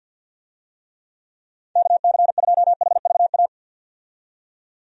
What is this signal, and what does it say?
Morse code, heard as a keyed tone, about 50 wpm, 700 Hz. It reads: KC2HVA
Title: USA Call Signs - 6 Characters - 50wpm